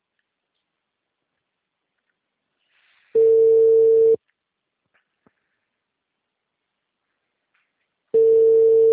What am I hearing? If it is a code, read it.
Ringback tone (China, 450 Hz).
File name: out- -2020- - - .19